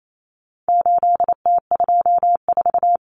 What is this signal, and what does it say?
Morse code, heard as a keyed tone, about 28 wpm, 700 Hz. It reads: 8T24
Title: Common Cut Numbers - 4-Digit 28wpm